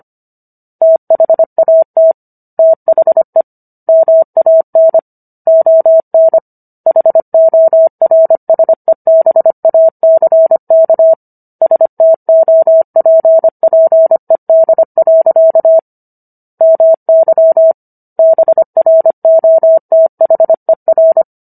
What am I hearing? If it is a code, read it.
Morse code, heard as a keyed tone, about 25 wpm, 650 Hz. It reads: THAT THE MAN ON HORSEBACK STOPPED. MY BROTHER